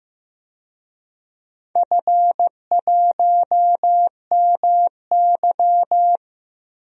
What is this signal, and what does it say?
Morse code, heard as a keyed tone, about 15 wpm, 700 Hz. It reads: F1MY
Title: DXCC-100 Call Signs - 4 Characters - Repeated 2x - 15wpm